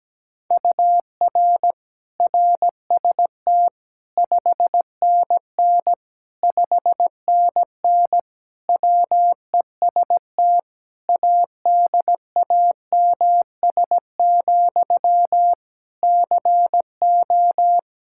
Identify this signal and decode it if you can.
Morse code, heard as a keyed tone, about 17 wpm, 700 Hz. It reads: UR RST 5NN 5NN WEST ADAMS, CO
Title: K1USN SST QSO Practice 17wpm